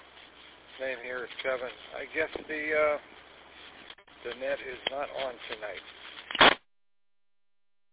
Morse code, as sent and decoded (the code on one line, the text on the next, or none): none